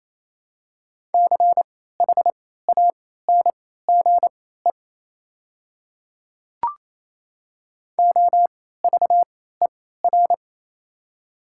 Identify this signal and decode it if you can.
Morse code, heard as a keyed tone, about 27 wpm, spaced as for 16 wpm, 700 Hz. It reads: CHANGE OVER
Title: 3x Character Spacing - Top 300 Words - 28wpm